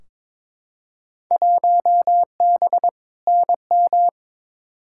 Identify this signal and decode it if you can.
Morse code, heard as a keyed tone, about 22 wpm, 700 Hz. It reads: 1B NM
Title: Field Day Exchanges Encore - 22wpm